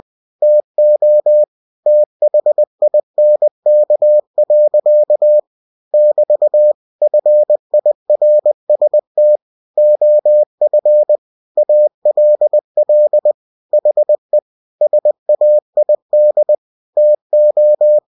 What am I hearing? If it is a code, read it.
Morse code, heard as a keyed tone, about 20 wpm, 600 Hz. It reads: TO THINK. <BT> FIRST OF ALL HE SAID TO
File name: cw-books / winnie-the-pooh / winnie-20wpm-page-0002